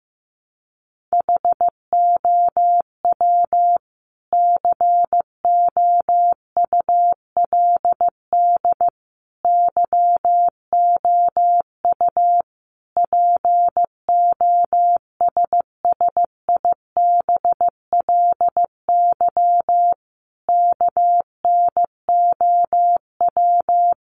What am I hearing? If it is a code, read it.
Morse code, heard as a keyed tone, about 15 wpm, 700 Hz. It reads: HOW COULD YOU POSSIBLY KNOW
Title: Sentences from Top 1300 Words 15wpm